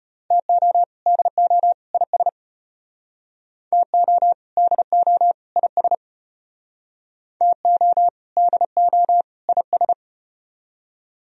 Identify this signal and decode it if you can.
Morse code, heard as a keyed tone, about 35 wpm, 700 Hz. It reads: TO DO IS TO DO IS TO DO IS